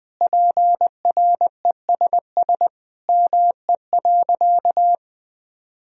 Morse code, heard as a keyed tone, about 20 wpm, 700 Hz. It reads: PRESS ME.
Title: Sentences from Top 1000 Words - Review Part 1 - 20wpm